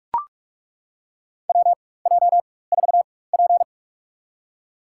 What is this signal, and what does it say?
Morse code, heard as a keyed tone, about 43 wpm, spaced as for 21 wpm, 700 Hz. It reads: WJ4P
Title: USA Call Signs - 4 Characters - 4x Character Spacing - 45wpm